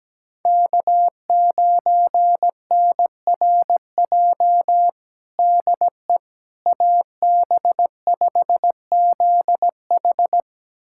Morse code, heard as a keyed tone, about 17 wpm, 700 Hz. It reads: K9NRJ DE AB5ZH
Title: USA Callsign DE Callsign - 5 Characters - 17wpm